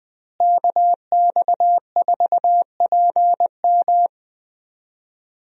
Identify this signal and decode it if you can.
Morse code, heard as a keyed tone, about 20 wpm, 700 Hz. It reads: KX4PM
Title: USA Call Signs - 5 Characters - Repeated 3x - 20wpm